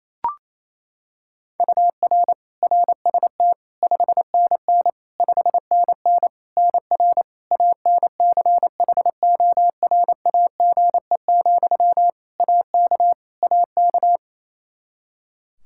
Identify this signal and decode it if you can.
Morse code, heard as a keyed tone, about 28 wpm, 700 Hz. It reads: UR RST 5NN 5NN NR ANCHORAGE, AK AK
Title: Practice QSO - Exchange 1 - Beginning - 28wpm